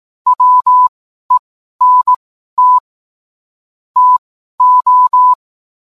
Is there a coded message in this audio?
Yes: Morse code from 0.3 to 5.3 s, heard as a keyed tone, about 18 wpm, spaced as for 12 wpm, 1000 Hz. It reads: WENT TO